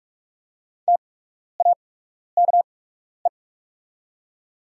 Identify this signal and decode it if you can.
Morse code, heard as a keyed tone, about 46 wpm, spaced as for 12 wpm, 700 Hz. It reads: TAKE
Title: 8x Character Spacing - Top 300 Words - 45wpm